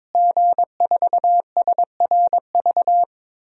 Morse code, heard as a keyed tone, about 22 wpm, 700 Hz. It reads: G4SRV
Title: DXCC-100 Call Signs - 5 Characters - Repeated 3x - 22wpm